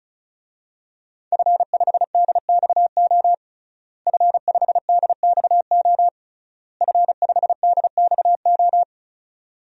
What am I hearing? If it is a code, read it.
Morse code, heard as a keyed tone, about 36 wpm, 700 Hz. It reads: F5DXO F5DXO F5DXO
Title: DXCC-100 Call Signs - 5 Characters - Repeated 3x - 35wpm